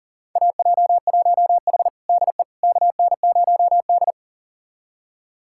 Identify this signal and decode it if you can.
Morse code, heard as a keyed tone, about 40 wpm, 700 Hz. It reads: AJ1H DE KN0D
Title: USA Callsign DE Callsign - 4 Characters - 40wpm